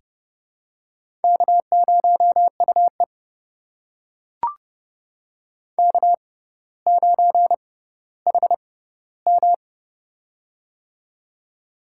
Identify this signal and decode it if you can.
Morse code, heard as a keyed tone, about 30 wpm, spaced as for 15 wpm, 700 Hz. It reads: K0UE K9HM
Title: USA Call Signs - 4 Characters - 6x Character Spacing - 30wpm